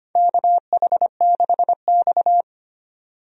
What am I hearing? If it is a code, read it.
Morse code, heard as a keyed tone, about 25 wpm, 700 Hz. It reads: KH6X
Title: DXCC-100 Call Signs - 4 Characters - 25wpm